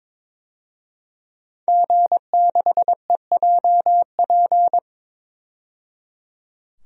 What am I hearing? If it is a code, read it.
Morse code, heard as a keyed tone, about 22 wpm, 700 Hz. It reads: G6EJP
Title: DXCC-100 Call Signs - 5 Characters - 22wpm